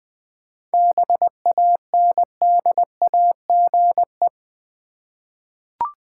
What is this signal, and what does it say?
Morse code, heard as a keyed tone, about 20 wpm, 700 Hz. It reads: BANDAGE
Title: Easy Words with Hints - 20wpm